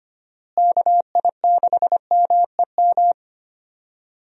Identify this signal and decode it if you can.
Morse code, heard as a keyed tone, about 25 wpm, 700 Hz. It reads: KI6MEM
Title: USA Call Signs - 6 Characters - Repeated 3x - 25wpm